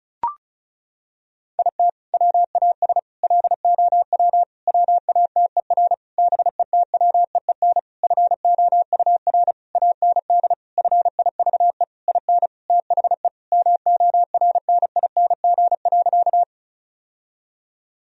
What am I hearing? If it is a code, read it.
Morse code, heard as a keyed tone, about 34 wpm, 700 Hz. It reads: IT WAS LOW WATER BETWEEN FOUR AND FIVE IN THE MORNING.
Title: Sentences from Top 1000 Words - Review Part 2 - 35wpm